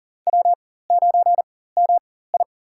Morse code, heard as a keyed tone, about 38 wpm, spaced as for 18 wpm, 700 Hz. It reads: W9MI
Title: USA Call Signs - 4 Characters - 4x Character Spacing - 40wpm